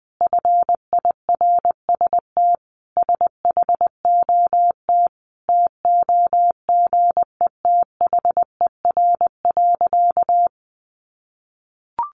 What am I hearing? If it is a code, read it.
Morse code, heard as a keyed tone, about 20 wpm, 700 Hz. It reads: FIRST SHOT TOGETHER.
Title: Sentences from Top 1100 Words 20wpm